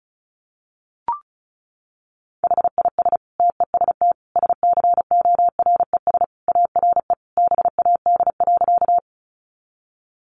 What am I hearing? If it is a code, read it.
Morse code, heard as a keyed tone, about 34 wpm, 700 Hz. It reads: HIS TEST SCORES ARE BAD.
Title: Additional Easy Phrases 35wpm